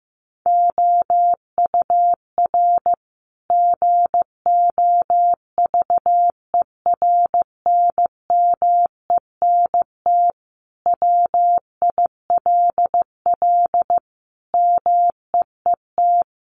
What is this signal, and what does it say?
Morse code, heard as a keyed tone, about 15 wpm, 700 Hz. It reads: OUR GOVERNMENT WILL MEET
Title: Sentences from Top 1200 Words 15wpm